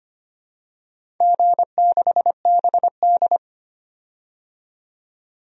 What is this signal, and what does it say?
Morse code, heard as a keyed tone, about 25 wpm, 700 Hz. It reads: G6BD
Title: DXCC-100 Call Signs - 4 Characters - 25wpm